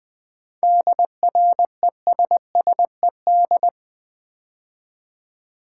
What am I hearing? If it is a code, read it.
Morse code, heard as a keyed tone, about 20 wpm, 700 Hz. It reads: DRESSED
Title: 7 Letter Words - First 3 Letters Given - 20wpm